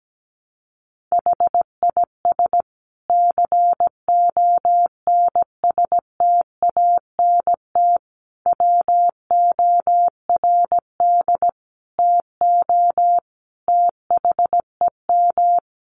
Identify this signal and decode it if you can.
Morse code, heard as a keyed tone, about 17 wpm, 700 Hz. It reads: HIS CONSTANT WORD TO THEM
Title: Sentences from Top 1500 Words 17wpm